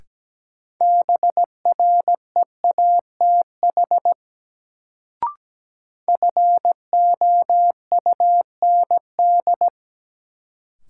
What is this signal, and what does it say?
Morse code, heard as a keyed tone, about 17 wpm, 700 Hz. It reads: BREATH FOUND
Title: Sets of 5 Words Encore - New Word Drill - 17wpm